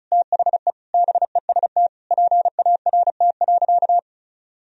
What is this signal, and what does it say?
Morse code, heard as a keyed tone, about 36 wpm, 700 Hz. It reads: THE BEST PART.
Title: Sentences from Top 1000 Words - Review Part 1 - 35wpm